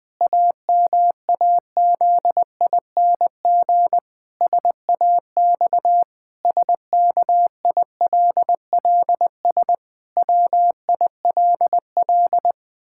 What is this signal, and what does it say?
Morse code, heard as a keyed tone, about 20 wpm, 700 Hz. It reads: AMAZING SAX SKILLS WILL
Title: Pangram Challenge v2 - 20wpm